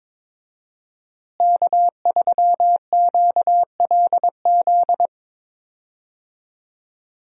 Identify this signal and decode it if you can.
Morse code, heard as a keyed tone, about 21 wpm, 700 Hz. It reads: K3QLZ
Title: USA Call Signs - 5 Characters - Repeated 2x - 22wpm